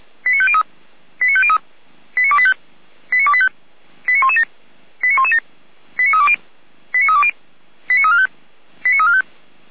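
Five-tone selective call (ZVEI), heard as quick runs of five tones, tones of about 70 ms. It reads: 89503 89503 89285 89285 89117 89117 89320 89320 89345 89345
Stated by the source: ZVEI five-tone call 89265 received by SDR on channel 455.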